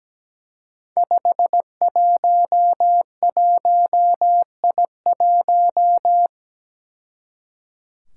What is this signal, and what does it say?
Morse code, heard as a keyed tone, about 17 wpm, 700 Hz. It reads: 511I1
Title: Five Challenging Letter-Number Combinations 17wpm